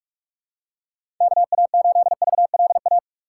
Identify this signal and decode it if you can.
Morse code, heard as a keyed tone, about 44 wpm, 700 Hz. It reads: KA8VLA
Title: USA Call Signs - 6x Character Spacing - 45wpm